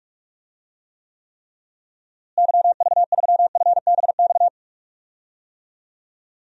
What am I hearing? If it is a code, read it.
Morse code, heard as a keyed tone, about 44 wpm, 700 Hz. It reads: YV3FBX
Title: DXCC-100 Call Signs - 6 Characters - Repeated 3x - 45wpm